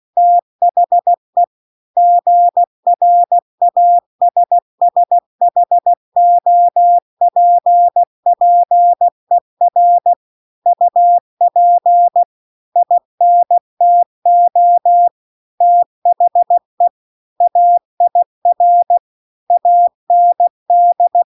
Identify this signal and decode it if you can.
Morse code, heard as a keyed tone, about 16 wpm, 700 Hz. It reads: THE GRASSHOPPER UP INTO THE AIR AND